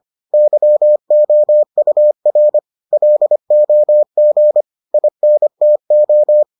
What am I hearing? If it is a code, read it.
Morse code, heard as a keyed tone, about 25 wpm, 600 Hz. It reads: YOUR LOG INTO